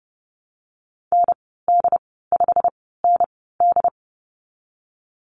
Morse code, heard as a keyed tone, about 32 wpm, 700 Hz. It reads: N D 5 N D
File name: Lesson 19g - Letter w - Call Signs 30wpm